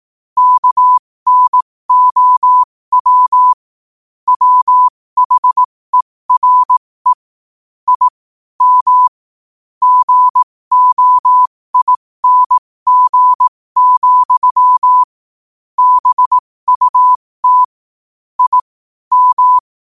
Morse code, heard as a keyed tone, about 18 wpm, spaced as for 15 wpm, 1000 Hz. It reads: KNOW WHERE I M GOING, BUT I M
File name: qotdfiles / 15wpm